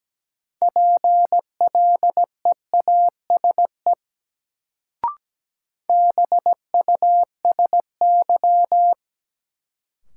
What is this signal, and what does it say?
Morse code, heard as a keyed tone, about 17 wpm, 700 Hz. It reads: PLEASE BUSY
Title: Sentences from Top 1500 Words - New Word Drill Alt - 17wpm